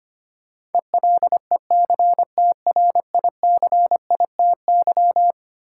Morse code, heard as a keyed tone, about 25 wpm, 700 Hz. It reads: ELECTRICITY
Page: morse-code-ninja-practice-25wpm